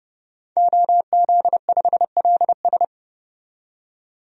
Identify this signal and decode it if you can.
Morse code, heard as a keyed tone, about 30 wpm, 700 Hz. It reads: OZ5LS
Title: DXCC-100 Call Signs - 5 Characters - 30wpm